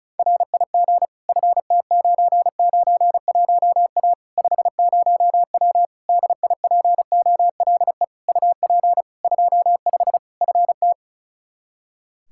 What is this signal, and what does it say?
Morse code, heard as a keyed tone, about 34 wpm, 700 Hz. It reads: RIG FT991A 50W DIPOLE UP 25 FT
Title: Practice QSO - Exchange 2 - Info - 35wpm